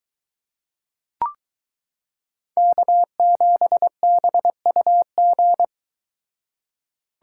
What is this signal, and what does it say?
Morse code, heard as a keyed tone, about 22 wpm, 700 Hz. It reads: K7BUG